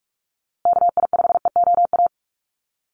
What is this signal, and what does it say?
Morse code, heard as a keyed tone, about 44 wpm, 700 Hz. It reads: KI5EOA
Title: USA Call Signs Repeated 3x 45wpm